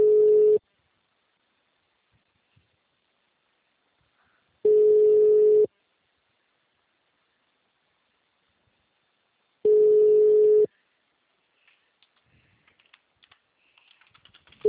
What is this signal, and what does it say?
Ringback tone (Europe, 425 Hz).
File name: out- -2016- - - .154